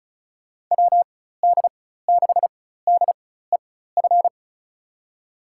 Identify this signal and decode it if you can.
Morse code, heard as a keyed tone, about 36 wpm, spaced as for 16 wpm, 700 Hz. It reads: WD6DEF
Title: USA Call Signs - 6 Characters - 4x Character Spacing - 35wpm